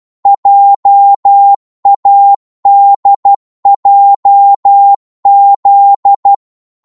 Morse code, heard as a keyed tone, about 12 wpm, 800 Hz. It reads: JADJZ
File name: Morse / ltr-12wpm